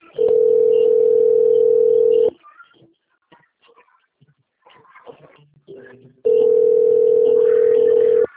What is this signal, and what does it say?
Ringback tone (North America), 440 + 480 Hz.